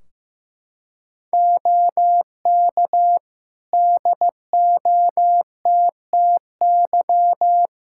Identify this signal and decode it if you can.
Morse code, heard as a keyed tone, about 15 wpm, 700 Hz. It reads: OK DOTTY
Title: Practice QSO - Exchange 4 - Ending - 15wpm